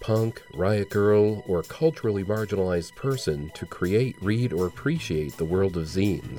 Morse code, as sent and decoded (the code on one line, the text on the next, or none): none